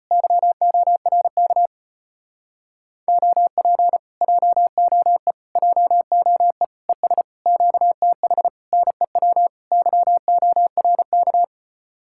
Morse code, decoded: YORK OP JOE JOE ES QTH NEW YORK

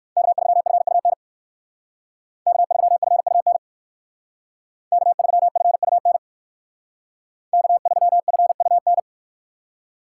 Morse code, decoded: K3FUN K3FUN K3FUN K3FUN

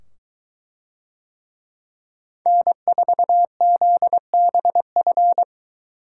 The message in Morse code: N4ZBF

